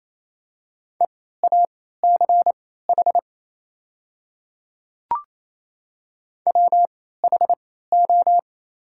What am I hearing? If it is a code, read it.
Morse code, heard as a keyed tone, about 28 wpm, spaced as for 16 wpm, 700 Hz. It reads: EACH WHO
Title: 3x Character Spacing - Top 100 Words - 28wpm